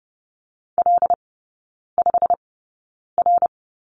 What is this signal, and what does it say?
Morse code, heard as a keyed tone, about 30 wpm, spaced as for 9 wpm, 700 Hz. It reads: L5R
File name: course / 30:10wpm / 087 Lesson 15e - Number 5 - Three Characters 30wpm